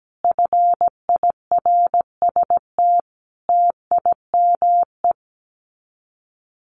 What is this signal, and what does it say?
Morse code, heard as a keyed tone, about 17 wpm, 700 Hz. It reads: FIRST TIME
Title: Sets of 5 Words 17wpm